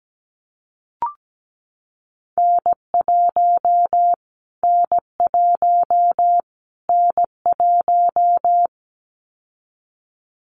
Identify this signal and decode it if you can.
Morse code, heard as a keyed tone, about 17 wpm, 700 Hz. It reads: N1 N1 N1
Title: Callsign Prefixes for Top 25 DXCC - Repeated 3x - 17wpm